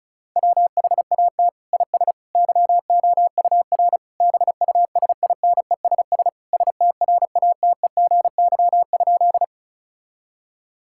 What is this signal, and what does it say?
Morse code, heard as a keyed tone, about 34 wpm, 700 Hz. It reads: WHAT IS YOUR BUSINESS STRATEGY?
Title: Sentences from Top 1000 Words - Review Part 5 - 35wpm